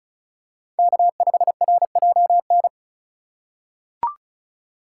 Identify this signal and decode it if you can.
Morse code, heard as a keyed tone, about 36 wpm, 700 Hz. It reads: K5RJN
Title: USA Call Signs - 5 Characters - 3x Character Spacing - 35wpm